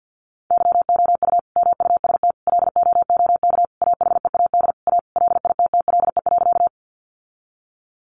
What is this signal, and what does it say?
Morse code, heard as a keyed tone, about 50 wpm, 700 Hz. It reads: YOU MUST LOOK AHEAD A LITTLE.
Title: Sentences from Top 1100 Words 50wpm